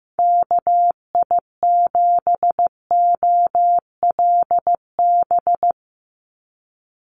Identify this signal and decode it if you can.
Morse code, heard as a keyed tone, about 15 wpm, 700 Hz. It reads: KI7OLB
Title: USA Call Signs 15wpm